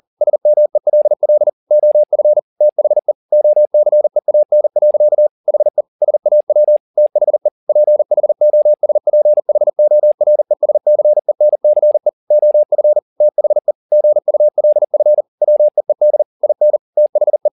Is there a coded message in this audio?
Yes: Morse code from 0.2 to 17.5 s, heard as a keyed tone, about 40 wpm, 600 Hz. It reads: SMELL OF THE OCEAN. HE SAW THE PHOSPHORESCENCE OF THE GULF WEED IN THE